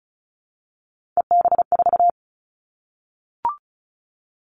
Morse code, heard as a keyed tone, about 34 wpm, 700 Hz. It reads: EB4
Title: All Cut Numbers - 3-Digit 35wpm